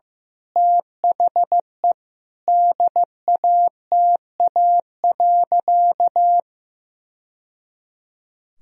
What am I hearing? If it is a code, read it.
Morse code, heard as a keyed tone, about 15 wpm, 700 Hz. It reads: THE DATA.
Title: Sentences from Top 1000 Words - Review Part 3 - 15wpm